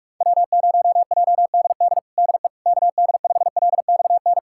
Morse code, heard as a keyed tone, about 43 wpm, 700 Hz. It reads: W0JDD DE KD5LXN
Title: USA Callsign DE Callsign 45wpm